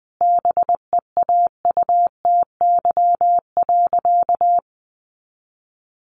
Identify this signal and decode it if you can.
Morse code, heard as a keyed tone, about 20 wpm, 700 Hz. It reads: BEAUTY.
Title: Sentences from Top 1200 Words 20wpm